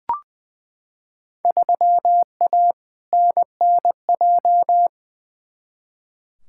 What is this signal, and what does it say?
Morse code, heard as a keyed tone, about 20 wpm, 700 Hz. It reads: 3A NNJ
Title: Field Day Exchanges Encore - 20wpm